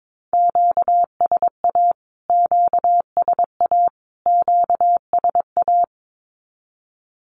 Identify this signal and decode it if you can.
Morse code, heard as a keyed tone, about 22 wpm, 700 Hz. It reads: QSA QSA QSA
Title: Q-Codes Repeated 3x 22wpm